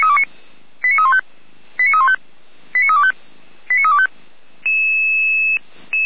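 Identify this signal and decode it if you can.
Five-tone selective call (ZVEI), heard as quick runs of five tones, tones of about 70 ms. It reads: 89315 89315 89325 89325
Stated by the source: ZVEI five-tone selective call received via SDR.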